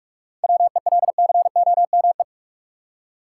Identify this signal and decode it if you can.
Morse code, heard as a keyed tone, about 43 wpm, 700 Hz. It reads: WELCOME
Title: Seven Letter Words - 45wpm